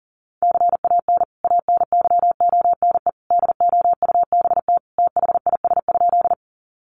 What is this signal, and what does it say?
Morse code, heard as a keyed tone, about 37 wpm, 700 Hz. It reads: CAN ANYONE DOUBT THIS?
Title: Sentences from Top 1500 Words 40wpm